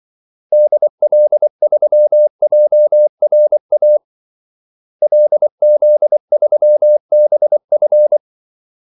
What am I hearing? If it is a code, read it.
Morse code, heard as a keyed tone, about 24 wpm, 600 Hz. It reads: DL3JRA LZ3BF
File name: Callsigns-24wpm